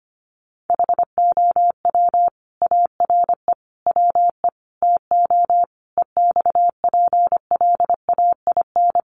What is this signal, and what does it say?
Morse code, heard as a keyed tone, about 25 wpm, 700 Hz. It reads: HOW ARE WE TO EXPLAIN
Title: Sentences from Top 1100 Words 25wpm